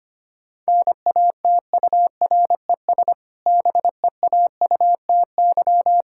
Morse code, heard as a keyed tone, about 25 wpm, 700 Hz. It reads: NATURES BEAUTY
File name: practice / 25wpm / Difficult Sentences with Hints - 25wpm